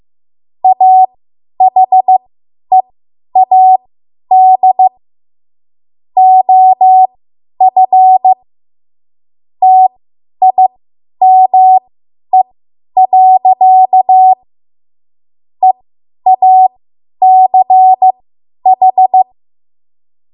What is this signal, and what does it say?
Morse code, heard as a keyed tone, about 15 wpm, spaced as for 10 wpm, 750 Hz. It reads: AHEAD OF TIME. EACH
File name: Morse / 10WPM